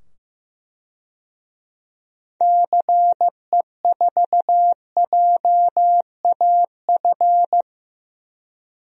Morse code, heard as a keyed tone, about 15 wpm, 700 Hz. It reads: CE4JAF
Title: DXCC-100 Call Signs - 6 Characters - 15wpm